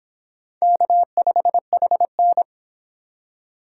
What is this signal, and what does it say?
Morse code, heard as a keyed tone, about 25 wpm, 700 Hz. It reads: K5HN